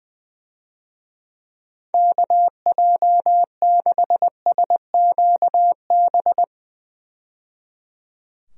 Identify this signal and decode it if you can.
Morse code, heard as a keyed tone, about 21 wpm, 700 Hz. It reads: KJ6SQB
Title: USA Call Signs - 6 Characters - 20wpm